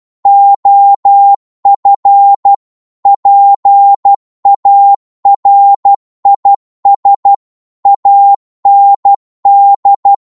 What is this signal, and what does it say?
Morse code, heard as a keyed tone, about 12 wpm, 800 Hz. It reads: OF PARIS AND